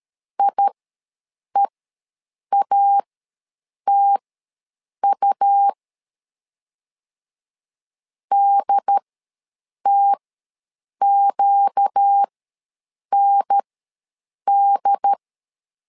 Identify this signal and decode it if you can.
Morse code, heard as a keyed tone, about 13 wpm, spaced as for 7 wpm, 800 Hz. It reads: IEATU DTQND
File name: LEZIONI / 4 Q ws 12wpm 1912